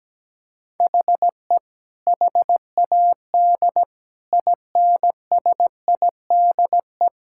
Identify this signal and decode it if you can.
Morse code, heard as a keyed tone, about 17 wpm, 700 Hz. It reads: HE HAD INSIDE
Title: Sentences from Top 1000 Words - Review Part 6 - 17wpm